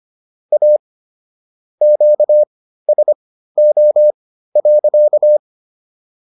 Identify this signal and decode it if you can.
Morse code, heard as a keyed tone, about 25 wpm, spaced as for 14 wpm, 600 Hz. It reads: A QSO.